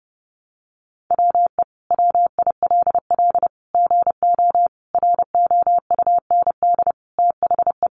Morse code, heard as a keyed tone, about 30 wpm, 700 Hz. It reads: WE WILL GO ROUND THE
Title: Sentences from Top 1300 Words 30wpm